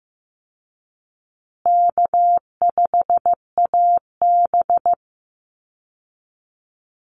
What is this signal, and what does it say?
Morse code, heard as a keyed tone, about 15 wpm, 700 Hz. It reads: K5AB
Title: Easy Call Signs Repeated 3x - 15wpm